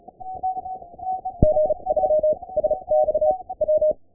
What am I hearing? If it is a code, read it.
Morse code, heard as a keyed tone, about 35 wpm, spaced as for 24 wpm, 610 Hz. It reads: G3SXW
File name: ke0ui / CW / pileup